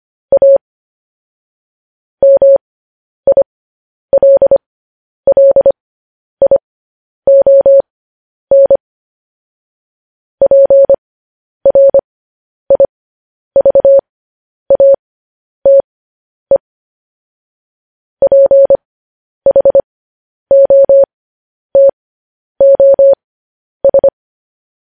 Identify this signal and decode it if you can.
Morse code, heard as a keyed tone, about 25 wpm, spaced as for 10 wpm, 550 Hz. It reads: A MILLION PRIVATE PHOTOS
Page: news-headlines-in-morse-code-at-10-wpm